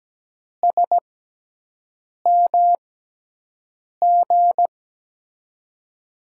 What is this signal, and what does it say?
Morse code, heard as a keyed tone, about 17 wpm, spaced as for 6 wpm, 700 Hz. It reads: SMG
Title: USA Call Signs - 5 Characters - 6x Character Spacing - 17wpm